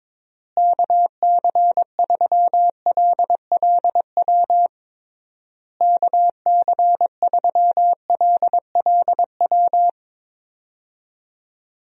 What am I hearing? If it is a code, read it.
Morse code, heard as a keyed tone, about 22 wpm, 700 Hz. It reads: KC3LLW KC3LLW
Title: USA Call Signs - 6 Characters - Repeated 3x - 22wpm